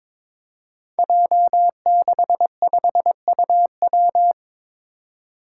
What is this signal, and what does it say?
Morse code, heard as a keyed tone, about 22 wpm, 700 Hz. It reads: J65UW
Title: Five Challenging Letter-Number Combinations 22wpm